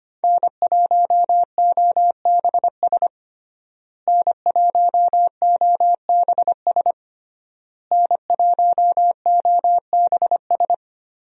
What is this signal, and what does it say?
Morse code, heard as a keyed tone, about 25 wpm, 700 Hz. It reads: N1OBS N1OBS N1OBS
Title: USA Call Signs - 5 Characters - Repeated 3x - 25wpm